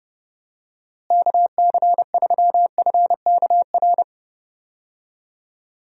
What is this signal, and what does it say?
Morse code, heard as a keyed tone, about 30 wpm, 700 Hz. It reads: KC3FKR
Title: USA Call Signs - 6 Characters - Repeated 2x - 30wpm